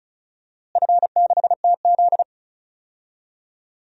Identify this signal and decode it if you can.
Morse code, heard as a keyed tone, about 36 wpm, 700 Hz. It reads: F6TZ
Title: DXCC-100 Call Signs - 4 Characters - Repeated 2x - 35wpm